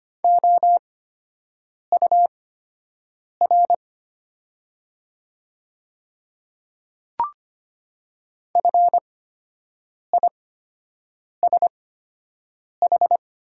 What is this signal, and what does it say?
Morse code, heard as a keyed tone, about 25 wpm, spaced as for 7 wpm, 690 Hz. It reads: OUR FISH
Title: 8x Character Spacing - Top 500 Words - 25wpm